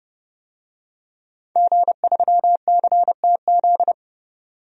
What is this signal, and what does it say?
Morse code, heard as a keyed tone, about 29 wpm, 700 Hz. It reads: G3CTZ
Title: DXCC-100 Call Signs - 5 Characters - 30wpm